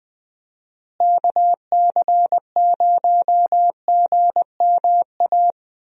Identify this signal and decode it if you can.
Morse code, heard as a keyed tone, about 21 wpm, 700 Hz. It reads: KC0GMA
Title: USA Call Signs - 6 Characters - Repeated 2x - 20wpm